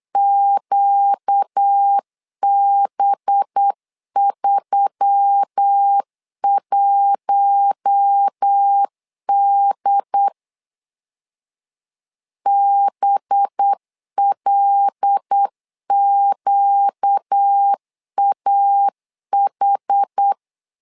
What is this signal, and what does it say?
Morse code, heard as a keyed tone, about 8 wpm, 800 Hz. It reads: QB31D BLQAH
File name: LEZIONI / 5 LBVC / 5 LB 8wpm 2816